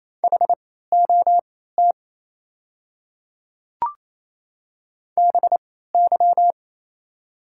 Morse code, heard as a keyed tone, about 27 wpm, spaced as for 16 wpm, 700 Hz. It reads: HOT BY
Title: 3x Character Spacing - Top 200 Words - 28wpm